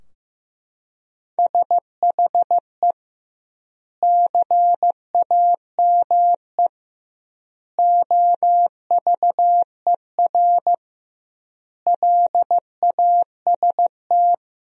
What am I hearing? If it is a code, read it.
Morse code, heard as a keyed tone, about 15 wpm, 700 Hz. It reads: SHE CAME OVER LAST
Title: 2x Word-Spacing - Sentences from Top 300 Words - 15wpm